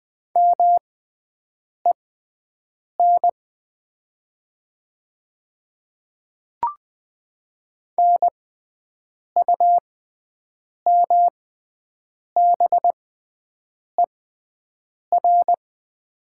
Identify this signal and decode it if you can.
Morse code, heard as a keyed tone, about 20 wpm, spaced as for 7 wpm, 700 Hz. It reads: MEN NUMBER
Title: 6x Character Spacing - Top 200 Words - 20wpm